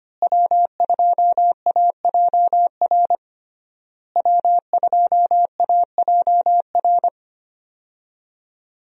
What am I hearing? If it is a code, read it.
Morse code, heard as a keyed tone, about 25 wpm, 700 Hz. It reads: W2AJR W2AJR
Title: USA Call Signs - 5 Characters - Repeated 3x - 25wpm